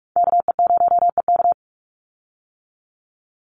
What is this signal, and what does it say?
Morse code, heard as a keyed tone, about 44 wpm, 700 Hz. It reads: KE0EK
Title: Call Signs - 45wpm